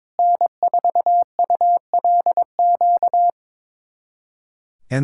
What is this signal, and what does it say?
Morse code, heard as a keyed tone, about 22 wpm, 700 Hz. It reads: N4ULQ
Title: USA Callsign DE Callsign - 5 Characters - 22wpm